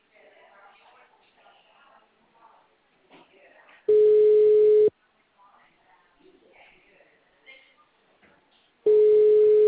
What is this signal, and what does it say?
Ringback tone (Europe, 425 Hz), with speech.